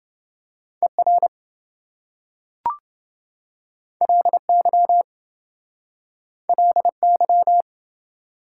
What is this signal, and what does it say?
Morse code, heard as a keyed tone, about 26 wpm, 700 Hz. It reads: ER LY LY